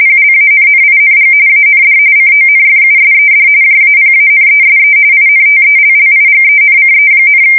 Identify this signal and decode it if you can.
RTTY, about 45.45 baud. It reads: PLANNING AND CONFUSED JAPANESE INTELLIGENCE,
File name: fsk170 / battleshipnj